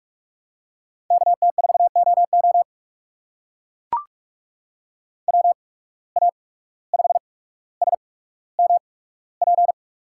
Morse code, heard as a keyed tone, about 44 wpm, spaced as for 18 wpm, 700 Hz. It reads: KT4OO WA5SMP
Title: USA Call Signs - 8x Character Spacing - 45wpm